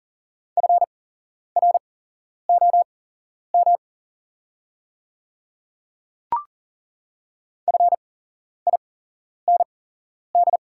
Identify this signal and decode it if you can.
Morse code, heard as a keyed tone, about 40 wpm, spaced as for 11 wpm, 690 Hz. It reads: FROM FIND
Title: 8x Character Spacing - Top 100 Words - 40wpm